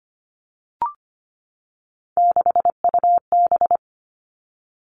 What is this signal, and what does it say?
Morse code, heard as a keyed tone, about 25 wpm, 700 Hz. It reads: 6UB
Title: All Cut Numbers - 3-Digit 25wpm